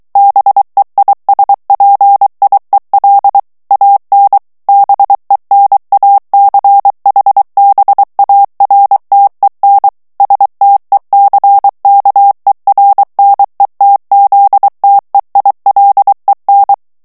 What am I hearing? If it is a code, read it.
Morse code, heard as a keyed tone, about 23 wpm, 800 Hz. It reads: BEISPIEL AN BENACHBARTEN STECKERNETZTEILEN
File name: morse mp3 / ALLEMAND texte clair 1 vitesse 22 mots minute